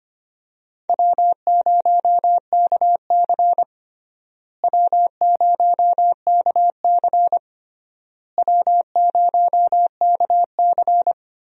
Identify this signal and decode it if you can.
Morse code, heard as a keyed tone, about 25 wpm, 700 Hz. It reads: W0KC W0KC W0KC
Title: USA Call Signs - 4 Characters - Repeated 3x - 25wpm